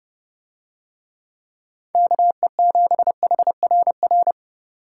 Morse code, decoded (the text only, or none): KE7HRR